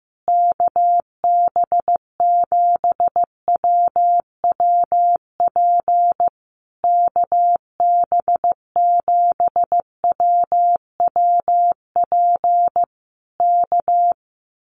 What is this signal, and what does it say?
Morse code, heard as a keyed tone, about 15 wpm, 700 Hz. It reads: KB7WWP KB7WWP K
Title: Common QSO Phrases Encore 15wpm